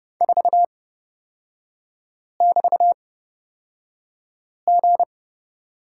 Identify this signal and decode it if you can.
Morse code, heard as a keyed tone, about 29 wpm, spaced as for 5 wpm, 700 Hz. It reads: 4<BT>G
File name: Lesson 36e - BT - Three Characters 30wpm